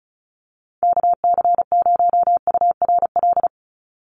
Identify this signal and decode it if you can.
Morse code, heard as a keyed tone, about 34 wpm, 700 Hz. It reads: KC0URL
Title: Call Signs - 35wpm